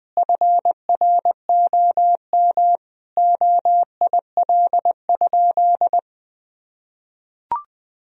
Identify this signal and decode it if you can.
Morse code, heard as a keyed tone, about 20 wpm, 700 Hz. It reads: FROM OIL?
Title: Sentences from Top 1000 Words - Review Part 4 - 20wpm